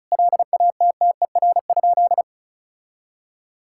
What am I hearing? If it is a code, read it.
Morse code, heard as a keyed tone, about 34 wpm, 700 Hz. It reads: LATTER?
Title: Sentences from Top 1600 Words 35wpm